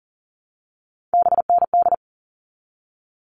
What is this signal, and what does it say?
Morse code, heard as a keyed tone, about 37 wpm, 700 Hz. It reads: BND
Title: All Cut Numbers - 3-Digit 40wpm